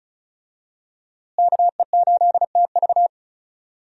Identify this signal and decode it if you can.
Morse code, heard as a keyed tone, about 34 wpm, 700 Hz. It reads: KE8TV
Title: USA Call Signs - 5 Characters - Repeated 2x - 35wpm